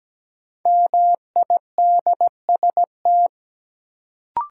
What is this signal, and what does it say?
Morse code, heard as a keyed tone, about 17 wpm, 700 Hz. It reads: MIDST
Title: Sets of 4 Words - New Word Drill - 17wpm